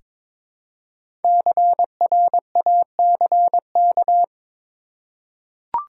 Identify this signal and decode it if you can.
Morse code, heard as a keyed tone, about 22 wpm, 700 Hz. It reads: CRACK
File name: 5 Letter Words - First 3 Letters Given - 22wpm